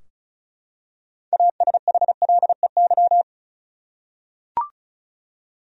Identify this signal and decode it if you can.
Morse code, heard as a keyed tone, about 34 wpm, 700 Hz. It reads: ASHLEY
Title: USA Common Names - 35wpm